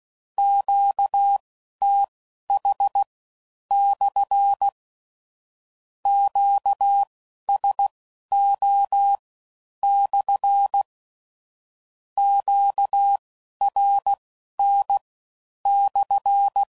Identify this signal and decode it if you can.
Morse code, heard as a keyed tone, about 16 wpm, spaced as for 11 wpm, 800 Hz. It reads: QTH/ QSO/ QRN/
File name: CW Code Groups @15WPM / Q Calls / Q Calls SPSL1